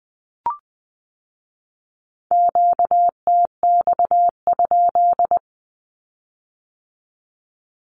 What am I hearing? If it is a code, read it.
Morse code, heard as a keyed tone, about 20 wpm, 700 Hz. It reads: QTX?
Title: Q-Codes with Spoken Meaning - 20wpm